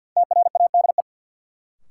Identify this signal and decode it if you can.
Morse code, heard as a keyed tone, about 52 wpm, 700 Hz. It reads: TRADE
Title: Sentences from Top 600 Words - New Word Drill - 50wpm